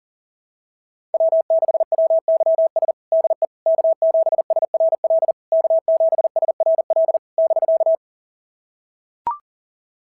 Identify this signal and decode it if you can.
Morse code, heard as a keyed tone, about 42 wpm, 690 Hz. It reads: W6WYS DE K7SRL K7SRL <BK>
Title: SOTA QSO Practice 40wpm